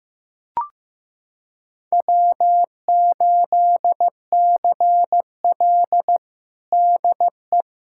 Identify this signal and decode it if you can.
Morse code, heard as a keyed tone, about 15 wpm, 700 Hz. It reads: W8CL DE
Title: USA Callsign DE Callsign - 4 Characters - 15wpm